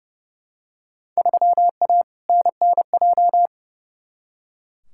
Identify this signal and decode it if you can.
Morse code, heard as a keyed tone, about 30 wpm, 700 Hz. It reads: 3A NNJ
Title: Field Day Exchanges - 30wpm